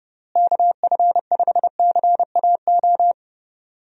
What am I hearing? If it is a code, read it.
Morse code, heard as a keyed tone, about 30 wpm, 700 Hz. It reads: KF5CAO
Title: USA Call Signs - 6 Characters - Repeated 3x - 30wpm